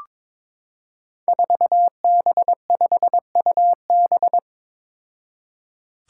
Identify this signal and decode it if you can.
Morse code, heard as a keyed tone, about 22 wpm, 700 Hz. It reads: 4B5UB
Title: Five Challenging Letter-Number Combinations 22wpm